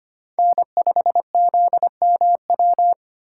Morse code, heard as a keyed tone, about 29 wpm, 700 Hz. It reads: N5ZMW
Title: Speed Racing - USA 1x3 Call Signs 25wpm